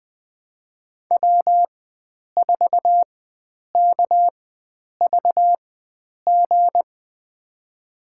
Morse code, heard as a keyed tone, about 20 wpm, spaced as for 9 wpm, 700 Hz. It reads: W4KVG
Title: USA Call Signs - 4x Character Spacing - 20wpm